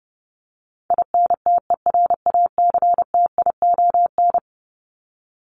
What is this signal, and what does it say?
Morse code, heard as a keyed tone, about 30 wpm, 700 Hz. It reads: INTERACTION
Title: Top 1251-1500 Words 30wpm